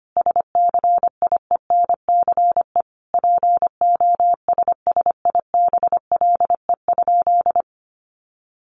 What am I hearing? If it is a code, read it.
Morse code, heard as a keyed tone, about 25 wpm, 700 Hz. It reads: SCIENCE POSSIBLE?
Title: Sentences from Top 1500 Words 25wpm